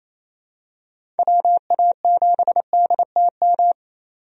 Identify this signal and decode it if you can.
Morse code, heard as a keyed tone, about 28 wpm, 700 Hz. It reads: WA7DTM
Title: USA Call Signs - 6 Characters - 28wpm